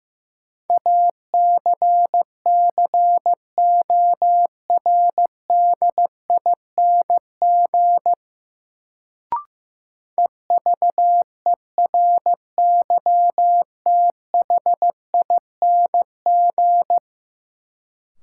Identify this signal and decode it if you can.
Morse code, heard as a keyed tone, about 15 wpm, 700 Hz. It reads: ACCORDING EVERYTHING
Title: Sentences from Top 700 Words - New Word Drill - 15wpm